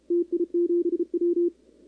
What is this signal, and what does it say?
Morse code, heard as a keyed tone, about 29 wpm, 340 Hz, with radio noise. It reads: TI7W
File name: contests / 2019-cw-wpx-kt5j-k5pi- -2330 0-11